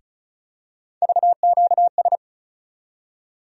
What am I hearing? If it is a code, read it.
Morse code, heard as a keyed tone, about 34 wpm, 700 Hz. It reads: VQS
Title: USA Call Sign Postfixes - 3 Letters 35wpm